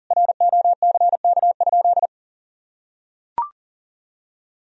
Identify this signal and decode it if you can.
Morse code, heard as a keyed tone, about 37 wpm, 700 Hz. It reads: ROCK?
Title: Sentences from Top 1000 Words - Review Part 2 - 40wpm